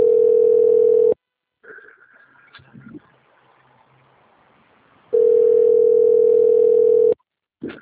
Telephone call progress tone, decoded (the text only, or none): ringback tone